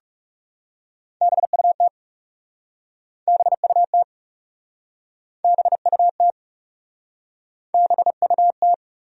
Morse code, heard as a keyed tone, about 40 wpm, 700 Hz. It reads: BUT BUT BUT BUT